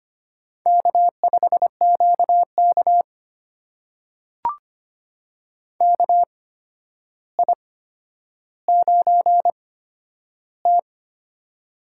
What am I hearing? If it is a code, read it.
Morse code, heard as a keyed tone, about 25 wpm, spaced as for 11 wpm, 700 Hz. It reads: K5QK KI9T
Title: USA Call Signs - 4 Characters - 8x Character Spacing - 25wpm